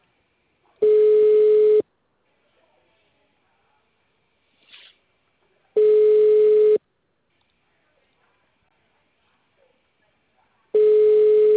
Ringback tone (Europe, 425 Hz).